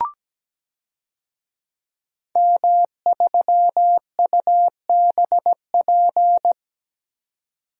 Morse code, heard as a keyed tone, about 17 wpm, 700 Hz. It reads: M3UBP
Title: DXCC-100 Call Signs - 5 Characters - 17wpm